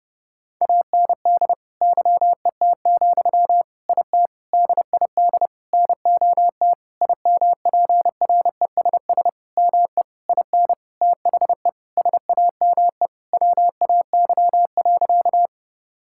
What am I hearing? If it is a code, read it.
Morse code, heard as a keyed tone, about 28 wpm, 700 Hz. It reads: AND YET, IT DID NOT IMPRESS ME IN THE SAME WAY.
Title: Sentences from Top 1600 Words 30wpm